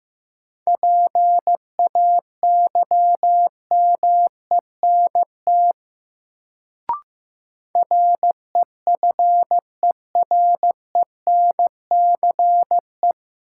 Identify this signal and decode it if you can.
Morse code, heard as a keyed tone, about 15 wpm, 700 Hz. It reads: PAYMENT REFERENCE
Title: Sentences from Top 1200 Words - New Word Drill Alt - 15wpm